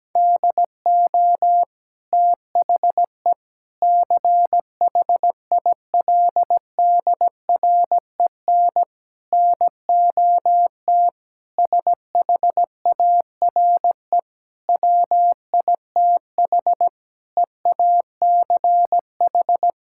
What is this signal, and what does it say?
Morse code, heard as a keyed tone, about 17 wpm, 700 Hz. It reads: DO THE CHILDREN NOT SHARE WITH EACH
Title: Sentences from Top 1000 Words - Review Part 6 - 17wpm